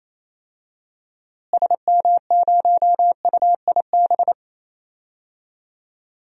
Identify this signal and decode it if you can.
Morse code, heard as a keyed tone, about 27 wpm, 700 Hz. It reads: SM0UIB
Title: DXCC-100 Call Signs - 6 Characters - Repeated 3x - 28wpm